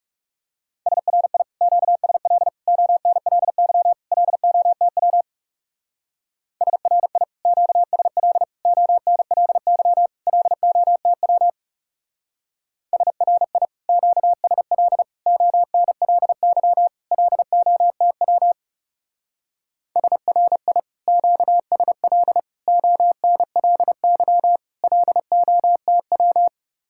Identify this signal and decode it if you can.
Morse code, heard as a keyed tone, about 36 wpm, 700 Hz. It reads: SRI QSL ONLY LOTW SRI QSL ONLY LOTW SRI QSL ONLY LOTW SRI QSL ONLY LOTW
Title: Speed-Racing Common QSO Phrases - 30wpm